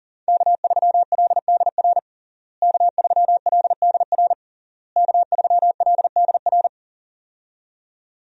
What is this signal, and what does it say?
Morse code, heard as a keyed tone, about 40 wpm, 700 Hz. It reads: K3LDR K3LDR K3LDR
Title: USA Call Signs - 5 Characters - Repeated 3x - 40wpm